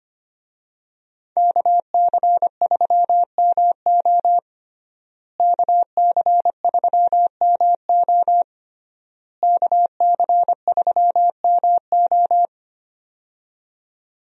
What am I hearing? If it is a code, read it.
Morse code, heard as a keyed tone, about 25 wpm, 700 Hz. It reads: KC3MO KC3MO KC3MO
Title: USA Call Signs - 5 Characters - Repeated 3x - 25wpm